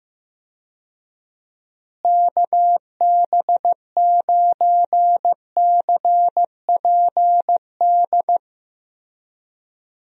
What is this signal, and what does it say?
Morse code, heard as a keyed tone, about 15 wpm, 700 Hz. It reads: KB9CPD
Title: USA Call Signs - 6 Characters - Repeated 3x - 15wpm